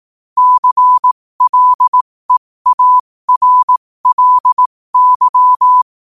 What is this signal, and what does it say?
Morse code, heard as a keyed tone, about 18 wpm, spaced as for 15 wpm, 1000 Hz. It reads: CLEARLY
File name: qotdfiles / 15wpm